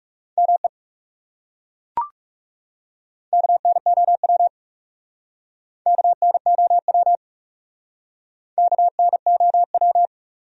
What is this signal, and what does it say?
Morse code, heard as a keyed tone, about 40 wpm, 700 Hz. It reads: ME KNOW KNOW KNOW